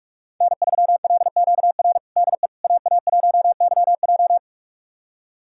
Morse code, heard as a keyed tone, about 46 wpm, 700 Hz. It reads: N3LQR DE AA1YJ
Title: USA Callsign DE Callsign - 5 Characters - 45wpm